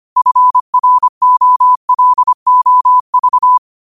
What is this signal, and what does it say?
Morse code, heard as a keyed tone, about 25 wpm, 1000 Hz. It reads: FROLOV